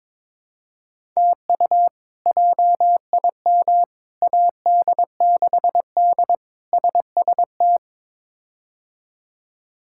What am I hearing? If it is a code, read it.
Morse code, heard as a keyed tone, about 22 wpm, 700 Hz. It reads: TU JIM AD6D SST
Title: K1USN SST QSO Practice - 22wpm